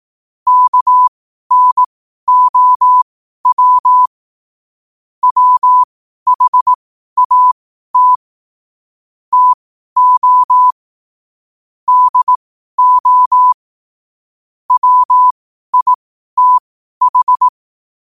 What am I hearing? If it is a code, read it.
Morse code, heard as a keyed tone, about 18 wpm, spaced as for 12 wpm, 1000 Hz. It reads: KNOW WHAT TO DO WITH
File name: qotdfiles / 12wpm